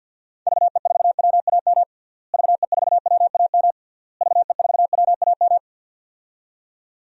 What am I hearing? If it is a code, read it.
Morse code, heard as a keyed tone, about 52 wpm, 700 Hz. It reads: VE4WAM VE4WAM VE4WAM
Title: DXCC-100 Call Signs - 6 Characters - Repeated 3x - 50wpm